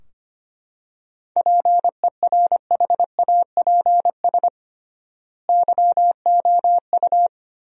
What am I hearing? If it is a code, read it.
Morse code, heard as a keyed tone, about 25 wpm, 700 Hz. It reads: PERHAPS YOU
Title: 3x Word-Spacing - Sentences from Top 500 Words - 25wpm